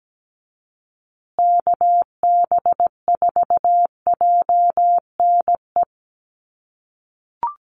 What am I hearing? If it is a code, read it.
Morse code, heard as a keyed tone, about 17 wpm, 700 Hz. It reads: KB4JNE
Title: USA Call Signs Repeated 3x 17wpm